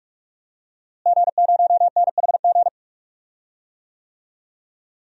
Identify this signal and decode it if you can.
Morse code, heard as a keyed tone, about 46 wpm, 700 Hz. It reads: G0NHG
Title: DXCC-100 Call Signs - 5 Characters - Repeated 2x - 45wpm